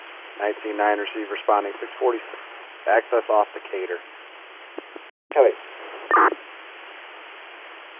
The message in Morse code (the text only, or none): none